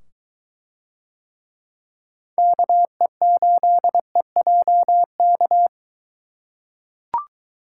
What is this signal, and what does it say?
Morse code, heard as a keyed tone, about 22 wpm, 700 Hz. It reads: KE8EJK